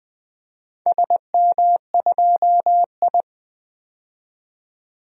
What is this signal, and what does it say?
Morse code, heard as a keyed tone, about 20 wpm, 700 Hz. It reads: SM2I
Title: DXCC-100 Call Signs - 4 Characters - 20wpm